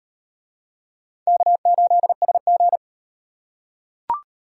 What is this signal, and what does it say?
Morse code, heard as a keyed tone, about 38 wpm, 700 Hz. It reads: K8SG